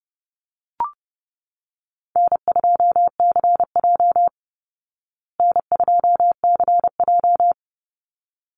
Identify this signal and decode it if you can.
Morse code, heard as a keyed tone, about 30 wpm, 700 Hz. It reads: N2CJ N2CJ
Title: Call Signs Repeated 3x - 30wpm